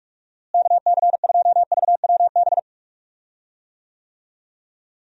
Morse code, heard as a keyed tone, about 46 wpm, 700 Hz. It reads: KC2VWB
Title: USA Call Signs - 6 Characters - Repeated 3x - 45wpm